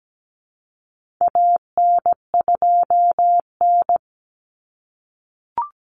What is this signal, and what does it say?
Morse code, heard as a keyed tone, about 17 wpm, 700 Hz. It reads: AN2N
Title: Common Cut Numbers - 4-Digit 17wpm